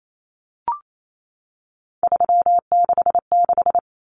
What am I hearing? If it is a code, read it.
Morse code, heard as a keyed tone, about 28 wpm, 700 Hz. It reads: 366
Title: Numbers - 3-Digit 28wpm